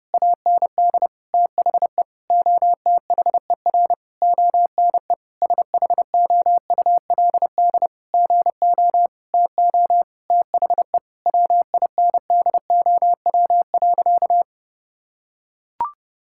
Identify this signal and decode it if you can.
Morse code, heard as a keyed tone, about 30 wpm, 700 Hz. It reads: AND THE OTHER ONE SHOULD GO TO THE WINDOW.
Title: Sentences from Top 1600 Words 30wpm